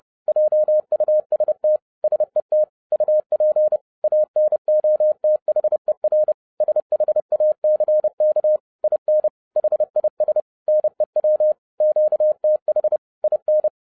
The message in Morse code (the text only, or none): JUST SET UP ANOTHER SHACK IN HIS NEW QTH IN